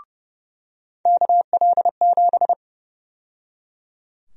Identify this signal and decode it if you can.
Morse code, heard as a keyed tone, about 30 wpm, 700 Hz. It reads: KL7
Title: USA Call Sign Prefixes - 30wpm